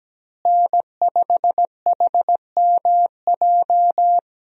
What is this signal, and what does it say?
Morse code, heard as a keyed tone, about 17 wpm, 700 Hz. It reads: N5HMJ
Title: USA Call Signs - 5 Characters - Repeated 3x - 17wpm